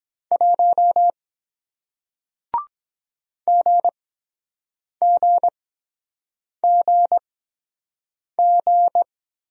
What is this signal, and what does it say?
Morse code, heard as a keyed tone, about 23 wpm, 700 Hz. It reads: 1 G G G G